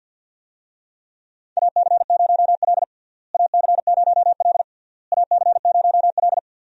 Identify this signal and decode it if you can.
Morse code, heard as a keyed tone, about 46 wpm, 700 Hz. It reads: AC0L AC0L AC0L
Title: USA Call Signs - 4 Characters - Repeated 3x - 50wpm